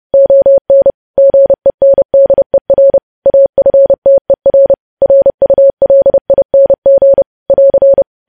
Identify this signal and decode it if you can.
Morse code, heard as a keyed tone, about 30 wpm, 550 Hz. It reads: ON GENDER AFTER RULING <AR>
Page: news headlines in morse code at 30 wpm